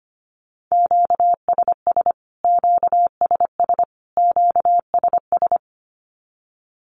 Morse code, heard as a keyed tone, about 25 wpm, 700 Hz. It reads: QSS QSS QSS
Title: Q-Codes Repeated 3x 25wpm